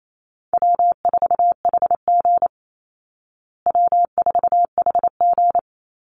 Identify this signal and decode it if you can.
Morse code, heard as a keyed tone, about 28 wpm, 700 Hz. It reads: W4HG W4HG
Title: Easy Call Signs Repeated 3x - 28wpm